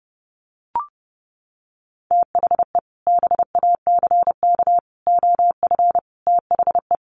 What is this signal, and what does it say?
Morse code, heard as a keyed tone, about 30 wpm, 700 Hz. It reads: THE BACK OF THE
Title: Sets of 5 Words Encore 30wpm